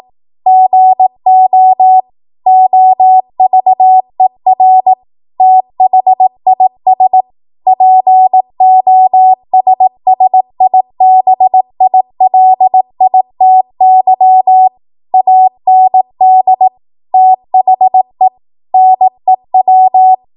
Morse code, heard as a keyed tone, about 18 wpm, 750 Hz. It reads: GO OVER THIS POSSIBILITY AND THE NEW
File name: Archive / 18 WPM / 18WPM